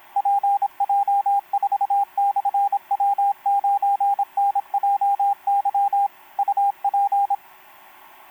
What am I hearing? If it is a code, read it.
Morse code, heard as a keyed tone, about 26 wpm, 800 Hz, with radio noise. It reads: PJ4/W9NJY UP
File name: pj4-w9njy f5upo etc 12m cw 28mar2011 1924